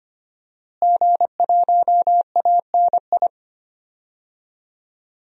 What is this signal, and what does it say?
Morse code, heard as a keyed tone, about 25 wpm, 700 Hz. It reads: G1ANI
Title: DXCC-100 Call Signs - 5 Characters - Repeated 2x - 25wpm